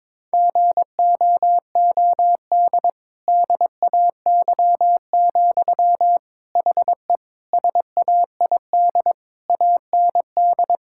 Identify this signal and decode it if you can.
Morse code, heard as a keyed tone, about 22 wpm, 700 Hz. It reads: GOOD DAY, HE SAID AND
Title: Sentences from Top 1600 Words 22wpm